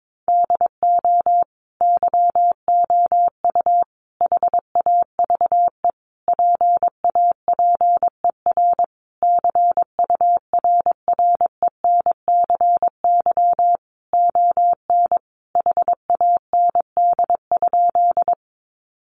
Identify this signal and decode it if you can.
Morse code, heard as a keyed tone, about 22 wpm, 700 Hz. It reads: DO YOU HAVE PAPER CURRENCY ON HAND?
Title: Sentences from Top 1500 Words 22wpm